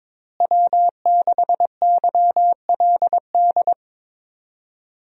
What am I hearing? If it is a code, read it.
Morse code, heard as a keyed tone, about 22 wpm, 700 Hz. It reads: W6YLD
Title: USA Call Signs - 5 Characters - Repeated 3x - 22wpm